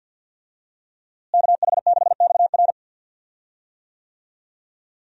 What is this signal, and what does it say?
Morse code, heard as a keyed tone, about 51 wpm, 700 Hz. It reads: KH6XR
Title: DXCC-100 Call Signs - 5 Characters - Repeated 3x - 50wpm